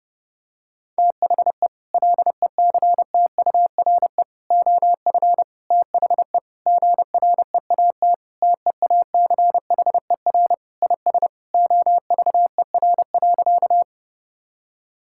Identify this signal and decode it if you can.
Morse code, heard as a keyed tone, about 32 wpm, 700 Hz. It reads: THE LECTURE OF THE GREAT TEACHER IS OVER.
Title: Sentences from Top 1600 Words 30wpm